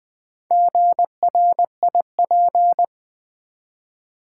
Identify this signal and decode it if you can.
Morse code, heard as a keyed tone, about 20 wpm, 690 Hz. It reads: GRIP
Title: Build A Word - 1 Letter At A Time - 4-Letter Words - 20wpm